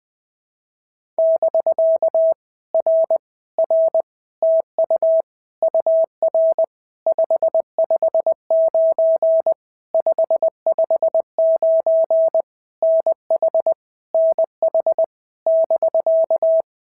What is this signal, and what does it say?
Morse code, heard as a keyed tone, about 20 wpm, 650 Hz. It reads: <BK> R R TU UR 559 559 NH NH <BK>
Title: POTA QSO Practice - 20wpm